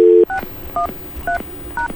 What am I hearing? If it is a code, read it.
DTMF, heard as telephone touch tones, tones of about 110 ms. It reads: 613#